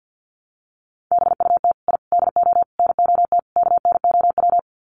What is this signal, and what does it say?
Morse code, heard as a keyed tone, about 50 wpm, 700 Hz. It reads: BUT I DO NOT KNOW
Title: Sets of 5 Words 50wpm